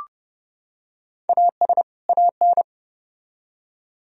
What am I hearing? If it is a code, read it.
Morse code, heard as a keyed tone, about 29 wpm, 700 Hz. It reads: AS AN